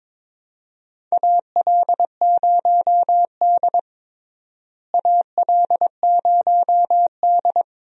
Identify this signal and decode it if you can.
Morse code, heard as a keyed tone, about 22 wpm, 700 Hz. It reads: AL0D AL0D
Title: USA Call Signs - 4 Characters - Repeated 2x - 22wpm